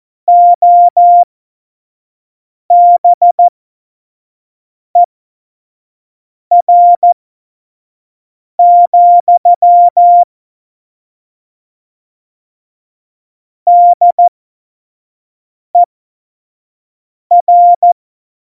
Morse code, heard as a keyed tone, about 14 wpm, spaced as for 5 wpm, 700 Hz. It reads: OBER, DER
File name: Morsekurs / Witze / Witz 13. August 2019, 25 F70